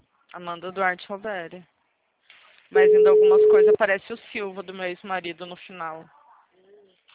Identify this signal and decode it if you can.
Ringback tone (Europe, 425 Hz), with speech.